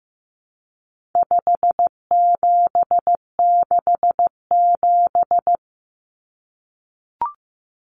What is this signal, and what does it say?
Morse code, heard as a keyed tone, about 15 wpm, 700 Hz. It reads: 5767
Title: Common Cut Numbers - 4-Digit 15wpm